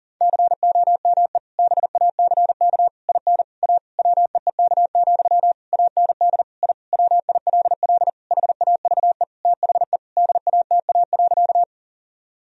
Morse code, heard as a keyed tone, about 40 wpm, 700 Hz. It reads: COME BACK IN A WEEK, AND I WILL HAVE THE DATA.
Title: Sentences from Top 1000 Words - Review Part 3 - 40wpm